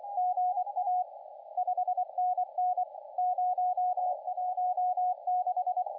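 Morse code, heard as a keyed tone, about 24 wpm, 720 Hz, with radio noise. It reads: GA 5NN 016
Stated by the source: CW contest audio recorded at LA4C during the SAC CW contest.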